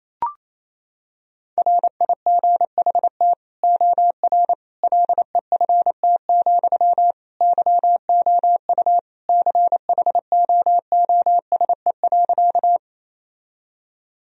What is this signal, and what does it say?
Morse code, heard as a keyed tone, about 27 wpm, 700 Hz. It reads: RIGHT OR LEFT, YOU CHOOSE.
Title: Sentences from Top 1000 Words - Review Part 4 - 28wpm